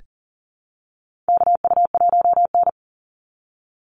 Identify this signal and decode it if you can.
Morse code, heard as a keyed tone, about 40 wpm, 700 Hz. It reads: KU1N
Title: Call Signs Repeated 2x - 40wpm